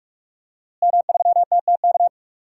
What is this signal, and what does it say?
Morse code, heard as a keyed tone, about 46 wpm, 700 Hz. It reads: M3TTK
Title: DXCC-100 Call Signs - 5 Characters - Repeated 2x - 45wpm